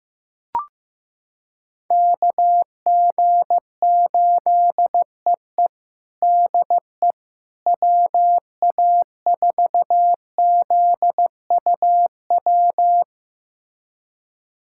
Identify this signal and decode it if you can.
Morse code, heard as a keyed tone, about 15 wpm, 700 Hz. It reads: KG8EE DE WA4ZUW
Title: USA Callsign DE Callsign 15wpm